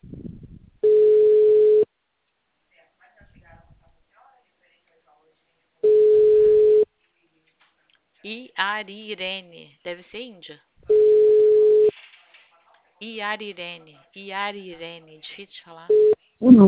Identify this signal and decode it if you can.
Ringback tone (Europe, 425 Hz), with speech.